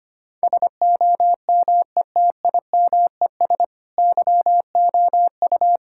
Morse code, heard as a keyed tone, about 25 wpm, 700 Hz. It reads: SOMETIMES YOU
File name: practice / 25wpm / Sentences with 3-Word Phrases - 25wpm